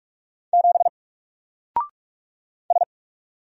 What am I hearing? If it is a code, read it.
Morse code, heard as a keyed tone, about 44 wpm, 700 Hz. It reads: 7 S
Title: Single Letter-Number Alt 45wpm